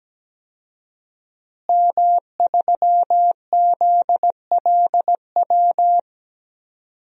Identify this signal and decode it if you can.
Morse code, heard as a keyed tone, about 17 wpm, 700 Hz. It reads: M3ZLW
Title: DXCC-100 Call Signs - 5 Characters - Repeated 2x - 17wpm